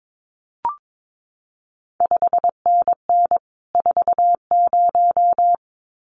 Morse code, heard as a keyed tone, about 22 wpm, 700 Hz. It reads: 5NN 40
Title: CQ WW DX Contest Exchange Encore - 22wpm